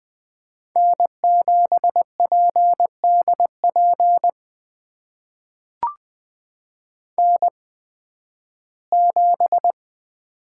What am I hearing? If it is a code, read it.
Morse code, heard as a keyed tone, about 20 wpm, spaced as for 13 wpm, 700 Hz. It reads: N7PDP N7
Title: USA Call Signs - 5 Characters - 8x Character Spacing - 20wpm